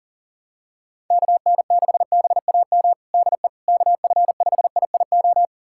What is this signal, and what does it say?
Morse code, heard as a keyed tone, about 40 wpm, 700 Hz. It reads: KN6BAM DE KF5IIO
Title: USA Callsign DE Callsign - 6 Characters - 40wpm